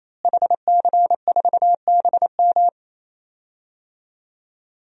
Morse code, heard as a keyed tone, about 28 wpm, 700 Hz. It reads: HC4BM
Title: DXCC-100 Call Signs - 5 Characters - Repeated 3x - 28wpm